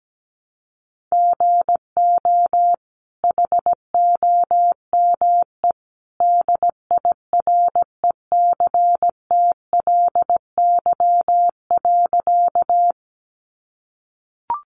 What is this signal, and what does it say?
Morse code, heard as a keyed tone, about 17 wpm, 700 Hz. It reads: GO HOME DIRECTLY.
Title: Sentences from Top 800 Words 17wpm